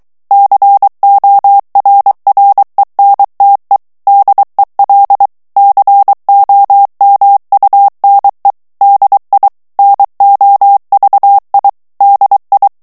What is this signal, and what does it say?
Morse code, heard as a keyed tone, about 23 wpm, 800 Hz. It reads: CORRENTE DEL COMUNE DI NOVI DI
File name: ITALIEN texte clair 1 vitesse 22 mots minute